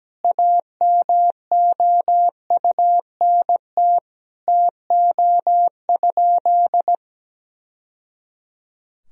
Morse code, heard as a keyed tone, about 17 wpm, 700 Hz. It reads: AMOUNT TO?
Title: Sentences from Top 1000 Words - Review Part 4 - 17wpm